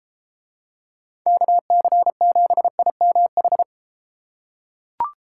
Morse code, heard as a keyed tone, about 33 wpm, 700 Hz. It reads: KC7IMH